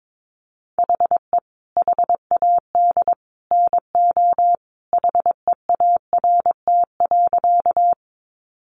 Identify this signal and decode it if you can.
Morse code, heard as a keyed tone, about 22 wpm, 700 Hz. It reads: HE HAD NO HEART.
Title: Sentences from Top 1200 Words 22wpm